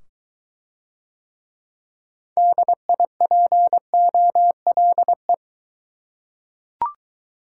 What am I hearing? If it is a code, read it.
Morse code, heard as a keyed tone, about 22 wpm, 700 Hz. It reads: DIPOLE